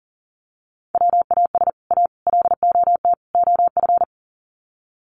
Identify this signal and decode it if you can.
Morse code, heard as a keyed tone, about 37 wpm, 700 Hz. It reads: WAS A LOT OF
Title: Sets of 4 Words 40wpm